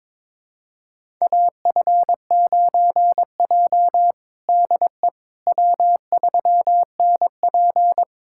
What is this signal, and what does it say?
Morse code, heard as a keyed tone, about 22 wpm, 700 Hz. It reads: AF9J DE W3NP
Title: USA Callsign DE Callsign - 4 Characters - 22wpm